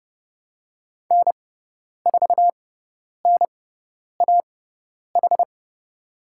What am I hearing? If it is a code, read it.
Morse code, heard as a keyed tone, about 31 wpm, spaced as for 10 wpm, 700 Hz. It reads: N4NAH
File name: Lesson 17g - Letter u - Call Signs 30wpm